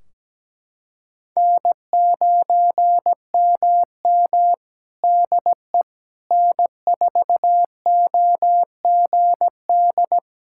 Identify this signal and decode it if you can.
Morse code, heard as a keyed tone, about 17 wpm, 700 Hz. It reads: N9MM DE N4OGD
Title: K1USN SST QSO Practice 17wpm